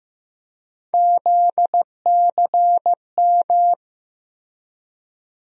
Morse code, heard as a keyed tone, about 15 wpm, 700 Hz. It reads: ZCM
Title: USA Call Sign Postfixes - 3 Letters 15wpm